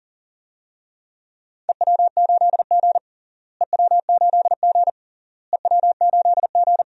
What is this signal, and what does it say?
Morse code, heard as a keyed tone, about 37 wpm, 700 Hz. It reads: EW8G EW8G EW8G
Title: DXCC-100 Call Signs - 4 Characters - Repeated 3x - 40wpm